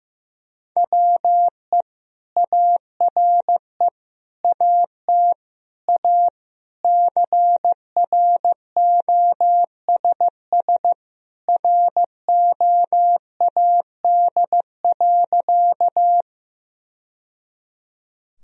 Morse code, heard as a keyed tone, about 16 wpm, 700 Hz. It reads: WE ARE AT A CROSS ROAD.
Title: Sentences from Top 1000 Words - Review Part 2 - 15wpm